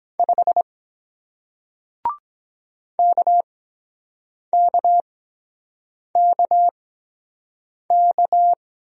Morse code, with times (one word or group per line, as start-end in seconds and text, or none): 0.2-0.6 5
3.0-8.5 KKKK